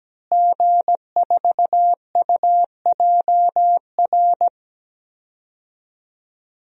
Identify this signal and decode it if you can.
Morse code, heard as a keyed tone, about 17 wpm, 700 Hz. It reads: G4UJR
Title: DXCC-100 Call Signs - 5 Characters - Repeated 2x - 17wpm